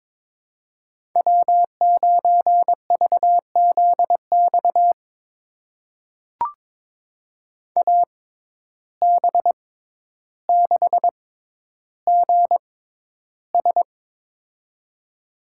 Morse code, heard as a keyed tone, about 22 wpm, spaced as for 11 wpm, 700 Hz. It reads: W9VZX AB6GS
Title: USA Call Signs - 6x Character Spacing - 22wpm